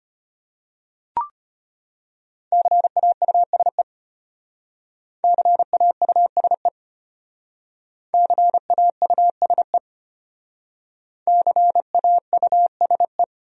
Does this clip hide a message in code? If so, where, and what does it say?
Yes: Morse code from 2.5 to 13.2 s, heard as a keyed tone, about 32 wpm, 700 Hz. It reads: CAUSE CAUSE CAUSE CAUSE